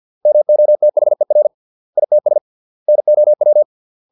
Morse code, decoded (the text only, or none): MOTHER ITS NOW